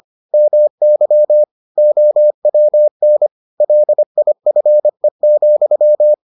Morse code, heard as a keyed tone, about 25 wpm, 600 Hz. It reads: MY OWN LIFE,